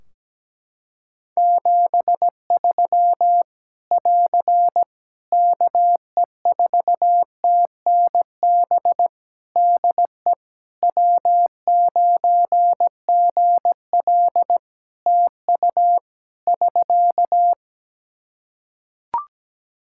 Morse code, heard as a keyed tone, about 17 wpm, 700 Hz. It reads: 73 <AR> KE4TNB DE W9GL TU <SK>
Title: K1USN SST QSO Practice 17wpm